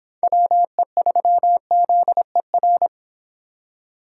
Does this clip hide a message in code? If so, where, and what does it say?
Yes: Morse code from 0.2 to 2.9 s, heard as a keyed tone, about 25 wpm, 700 Hz. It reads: WE3ZER